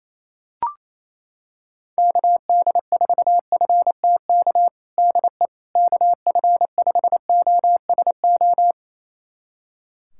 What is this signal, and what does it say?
Morse code, heard as a keyed tone, about 28 wpm, 700 Hz. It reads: KD4FTK DE KF5OSO
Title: USA Callsign DE Callsign 28wpm